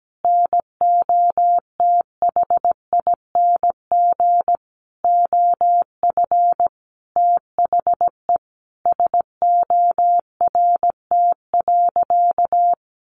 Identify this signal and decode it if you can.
Morse code, heard as a keyed tone, about 17 wpm, 700 Hz. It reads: NOTHING OF THE SORT.
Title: Sentences from Top 1300 Words 17wpm